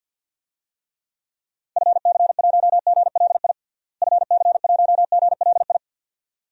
Morse code, heard as a keyed tone, about 47 wpm, 700 Hz. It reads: FC1GLI FC1GLI
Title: DXCC-100 Call Signs - 6 Characters - Repeated 2x - 50wpm